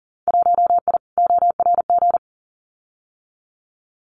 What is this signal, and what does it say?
Morse code, heard as a keyed tone, about 39 wpm, 700 Hz. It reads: 1I ORG
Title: Winter Field Day Exchanges Encore 40wpm